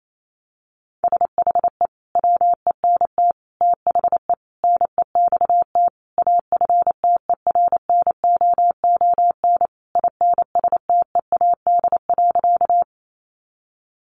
Morse code, heard as a keyed tone, about 28 wpm, 700 Hz. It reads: SHE WENT THE NEXT AFTERNOON INSTEAD.
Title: Sentences from Top 1500 Words 28wpm